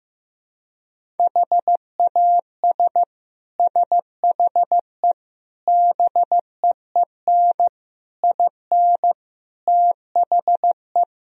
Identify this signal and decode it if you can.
Morse code, heard as a keyed tone, about 15 wpm, 700 Hz. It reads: HAS SHE BEEN IN THE
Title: Sentences from Top 1000 Words - Review Part 4 - 15wpm